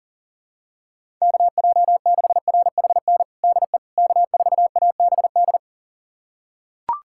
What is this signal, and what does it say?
Morse code, heard as a keyed tone, about 37 wpm, 700 Hz. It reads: KJ6RHN DE K4ABD
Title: USA Callsign DE Callsign 40wpm